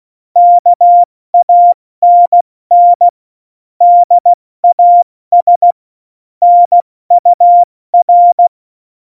Morse code, heard as a keyed tone, about 16 wpm, 700 Hz. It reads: KANN DAS NUR
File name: Morsekurs / Witze / Witz 13 March 2018, 70 F80